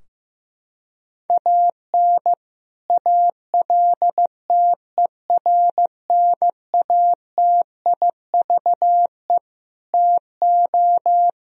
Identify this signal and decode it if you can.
Morse code, heard as a keyed tone, about 15 wpm, 700 Hz. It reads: AN ALTERNATIVE TO
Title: Sentences from Top 1000 Words - Review Part 4 - 15wpm